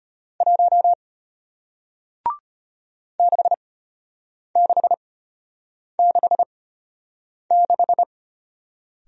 Morse code, heard as a keyed tone, about 33 wpm, spaced as for 8 wpm, 700 Hz. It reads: 1 6666